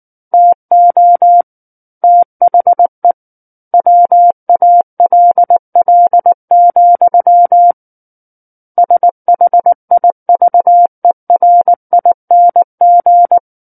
Morse code, heard as a keyed tone, about 19 wpm, 700 Hz. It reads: TO THE WALL, SHIVERING
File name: Book One / 10-Chapter 10 - 19 WPM